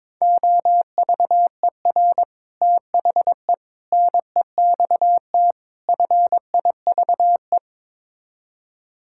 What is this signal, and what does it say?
Morse code, heard as a keyed tone, about 21 wpm, 700 Hz. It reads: OVER THE NEXT FIVE
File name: practice / 22wpm / Build A Phrase - 5-Word Phrases - 22wpm